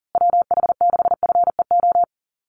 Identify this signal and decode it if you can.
Morse code, heard as a keyed tone, about 40 wpm, 700 Hz. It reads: WH6FEO
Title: USA Call Signs Repeated 3x 40wpm